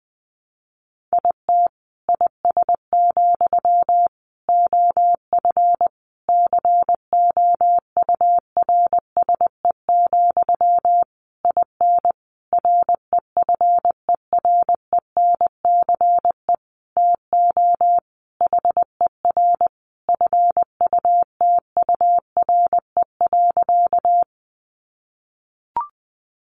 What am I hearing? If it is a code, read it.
Morse code, heard as a keyed tone, about 20 wpm, 700 Hz. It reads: IT IS, OF COURSE, IN REFERENCE TO HER FUTURE.
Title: Sentences from Top 1200 Words 20wpm